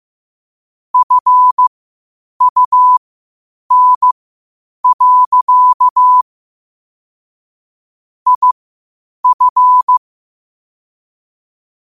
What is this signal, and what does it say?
Morse code, heard as a keyed tone, about 15 wpm, spaced as for 8 wpm, 1000 Hz. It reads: FUN. IF
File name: qotdfiles / 8wpm